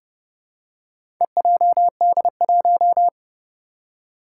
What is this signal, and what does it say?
Morse code, heard as a keyed tone, about 30 wpm, 700 Hz. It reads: EJD1
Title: Four Challenging Letter-Number Combinations 30wpm